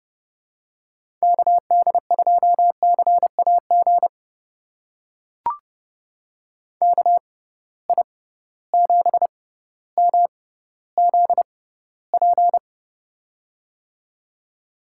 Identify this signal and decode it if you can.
Morse code, heard as a keyed tone, about 30 wpm, spaced as for 15 wpm, 700 Hz. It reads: KD2CAG KI7MZP
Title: USA Call Signs - 6 Characters - 6x Character Spacing - 30wpm